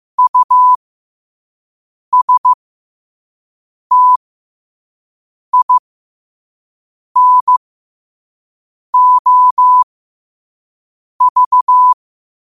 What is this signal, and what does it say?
Morse code, heard as a keyed tone, about 15 wpm, spaced as for 5 wpm, 1000 Hz. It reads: USTINOV
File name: qotdfiles / 5wpm